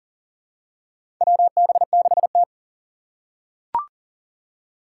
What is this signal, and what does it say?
Morse code, heard as a keyed tone, about 37 wpm, 700 Hz. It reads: WB6T
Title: USA Call Signs - 3x Character Spacing - 40wpm